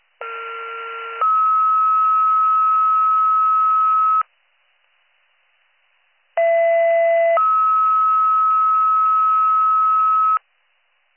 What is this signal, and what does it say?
Two-tone page, heard as a one-second tone, then a three-second tone, about 1370 then 1290 Hz, about 670 then 1290 Hz. It is heard from 0.2 to 10.4 s.